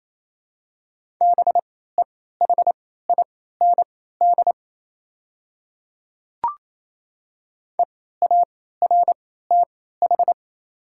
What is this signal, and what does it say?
Morse code, heard as a keyed tone, about 28 wpm, spaced as for 16 wpm, 690 Hz. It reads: BEHIND EARTH
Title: 3x Character Spacing - Top 500 Words - 28wpm